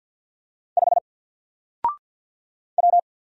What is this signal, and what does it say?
Morse code, heard as a keyed tone, about 50 wpm, 700 Hz. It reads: 5 W
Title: Single Letter-Number 50wpm